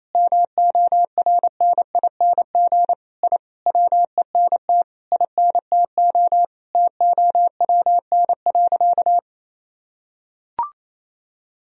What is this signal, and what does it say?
Morse code, heard as a keyed tone, about 28 wpm, 700 Hz. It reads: MORNING I WENT INTO TOWN.
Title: Sentences from Top 1000 Words - Review Part 2 - 28wpm